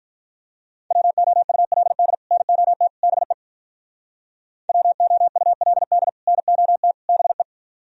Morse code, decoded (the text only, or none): WOULD NOT BE WOULD NOT BE